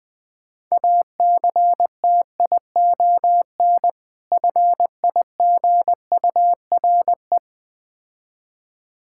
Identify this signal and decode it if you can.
Morse code, heard as a keyed tone, about 20 wpm, 700 Hz. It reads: ACTION FIGURE
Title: Difficult Words with Hints - 20wpm